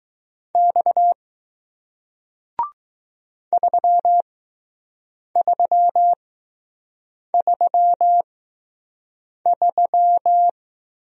Morse code, heard as a keyed tone, about 20 wpm, spaced as for 6 wpm, 700 Hz. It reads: X 3333